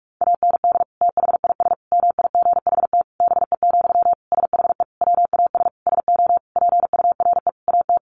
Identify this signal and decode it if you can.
Morse code, heard as a keyed tone, about 44 wpm, 700 Hz. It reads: AND THIS MIGHT BE, SHE WAS SO PURE AT
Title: Sentences from Top 1500 Words 45wpm